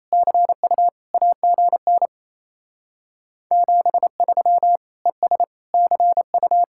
Morse code, heard as a keyed tone, about 34 wpm, 700 Hz. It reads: CU AGN 73 ES CU
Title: Speed-Racing Common QSO Phrases - 28wpm